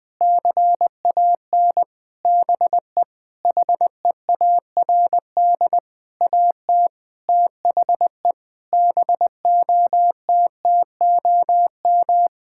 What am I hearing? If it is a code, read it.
Morse code, heard as a keyed tone, about 19 wpm, 700 Hz. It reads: CAN BE HEARD AT THE BOTTOM
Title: Sentences from Top 1000 Words - Review Part 6 - 20wpm